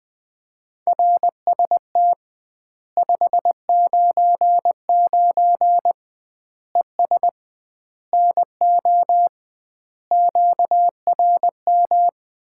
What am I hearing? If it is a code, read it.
Morse code, heard as a keyed tone, about 21 wpm, 700 Hz. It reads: RST 599 ES NO QRM
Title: Common QSO Phrases - 2x Word-Spacing - 20wpm